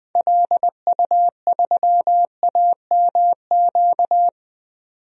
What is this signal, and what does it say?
Morse code, heard as a keyed tone, about 21 wpm, 700 Hz. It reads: LU3AMQ
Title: DXCC-100 Call Signs - 6 Characters - 20wpm